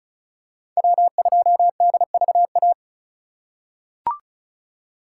Morse code, heard as a keyed tone, about 34 wpm, 700 Hz. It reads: W2DVA
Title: USA Call Signs - 5 Characters - 4x Character Spacing - 35wpm